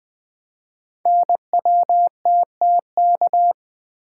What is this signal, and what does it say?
Morse code, heard as a keyed tone, about 20 wpm, 700 Hz. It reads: NWTTK
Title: Code Groups - 5 Letters - 20wpm